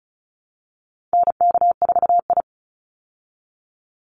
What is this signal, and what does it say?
Morse code, heard as a keyed tone, about 34 wpm, 700 Hz. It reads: NK4I
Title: Call Signs Repeated 2x - 35wpm